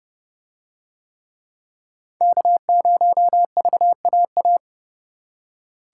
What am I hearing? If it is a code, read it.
Morse code, heard as a keyed tone, about 28 wpm, 700 Hz. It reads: K0VAA